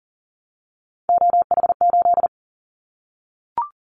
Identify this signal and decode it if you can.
Morse code, heard as a keyed tone, about 40 wpm, 700 Hz. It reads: OH8
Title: Callsign Prefixes for Top 25 DXCC - 40wpm